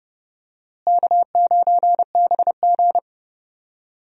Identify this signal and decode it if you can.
Morse code, heard as a keyed tone, about 30 wpm, 700 Hz. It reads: K9BG
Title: USA Call Signs - 4 Characters - Repeated 2x - 30wpm